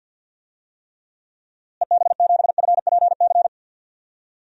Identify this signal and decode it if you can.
Morse code, heard as a keyed tone, about 46 wpm, 700 Hz. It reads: EB7FPC
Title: DXCC-100 Call Signs - 6 Characters - Repeated 3x - 50wpm